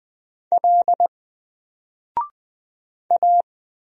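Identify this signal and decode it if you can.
Morse code, heard as a keyed tone, about 20 wpm, 700 Hz. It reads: L A
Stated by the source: Single letter and number Morse drill at 20 wpm, from N4LCD.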